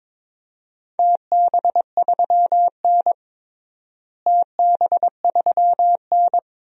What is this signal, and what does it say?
Morse code, heard as a keyed tone, about 22 wpm, 700 Hz. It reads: TB3N TB3N
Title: DXCC-100 Call Signs - 4 Characters - Repeated 2x - 22wpm